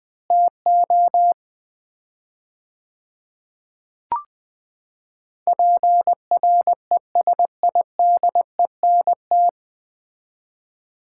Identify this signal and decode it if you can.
Morse code, heard as a keyed tone, about 21 wpm, 700 Hz. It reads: TO PRESIDENT
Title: 4x Word-Spacing - Sets of 4 Words - 20wpm